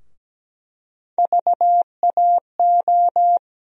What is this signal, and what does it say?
Morse code, heard as a keyed tone, about 17 wpm, 700 Hz. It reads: VAO
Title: USA Call Sign Postfixes - 3 Letters 17wpm